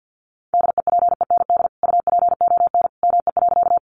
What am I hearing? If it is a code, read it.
Morse code, heard as a keyed tone, about 50 wpm, 700 Hz. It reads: DEPEND UPON ME.
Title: Sentences from Top 1300 Words 50wpm